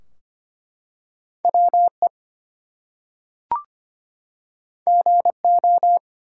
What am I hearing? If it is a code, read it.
Morse code, heard as a keyed tone, about 25 wpm, 700 Hz. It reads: WE GO
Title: Two Letter Words - 25wpm